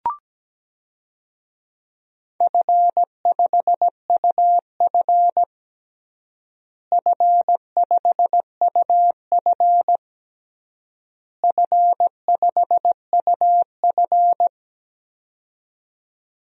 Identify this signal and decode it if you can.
Morse code, heard as a keyed tone, about 17 wpm, 700 Hz. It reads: F5UF F5UF F5UF
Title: DXCC-100 Call Signs - 4 Characters - Repeated 3x - 17wpm